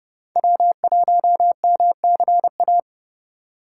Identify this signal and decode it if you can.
Morse code, heard as a keyed tone, about 30 wpm, 700 Hz. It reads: W1MCA
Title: USA Call Signs - 5 Characters - 2x Character Spacing - 30wpm